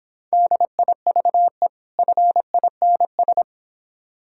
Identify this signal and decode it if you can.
Morse code, heard as a keyed tone, about 25 wpm, 700 Hz. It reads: DIVE FINS